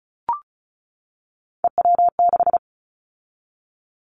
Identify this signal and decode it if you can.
Morse code, heard as a keyed tone, about 36 wpm, 700 Hz. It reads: EW6
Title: Callsign Prefixes for Top 50 DXCC - 35wpm